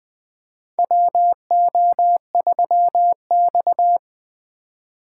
Morse code, heard as a keyed tone, about 20 wpm, 700 Hz. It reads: WO3X
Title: USA Call Signs - 4 Characters - 4x Character Spacing - 20wpm